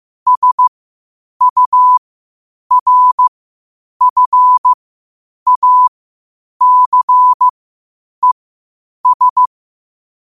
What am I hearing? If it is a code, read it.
Morse code, heard as a keyed tone, about 15 wpm, spaced as for 8 wpm, 1000 Hz. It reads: SURFACES
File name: qotdfiles / 8wpm